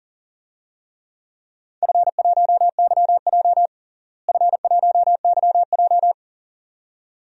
Morse code, heard as a keyed tone, about 40 wpm, 700 Hz. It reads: F1YJ F1YJ
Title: DXCC-100 Call Signs - 4 Characters - Repeated 2x - 40wpm